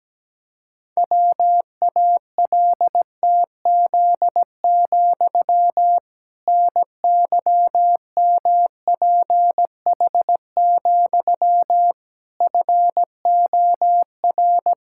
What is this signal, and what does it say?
Morse code, heard as a keyed tone, about 17 wpm, 700 Hz. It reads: WALTZ, NYMPH, FOR
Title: Pangram Challenge v2 - 17wpm